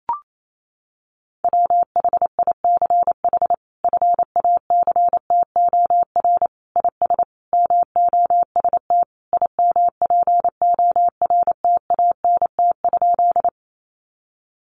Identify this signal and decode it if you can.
Morse code, heard as a keyed tone, about 28 wpm, 700 Hz. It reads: WHICH FACTOR IS MOST IMPORTANT?
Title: Sentences from Top 1300 Words 28wpm